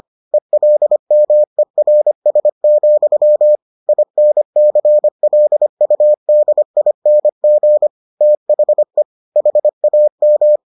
Morse code, heard as a keyed tone, about 25 wpm, 600 Hz. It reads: ELMERS, INCLUDING THE HAM